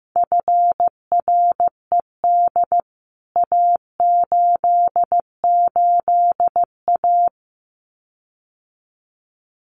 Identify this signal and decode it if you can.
Morse code, heard as a keyed tone, about 15 wpm, 700 Hz. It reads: FRED A88A
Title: CWT Exchange 15wpm